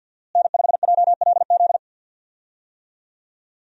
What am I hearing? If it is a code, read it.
Morse code, heard as a keyed tone, about 53 wpm, 700 Hz. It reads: N5JLZ
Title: USA Call Signs - 5 Characters - Repeated 2x - 50wpm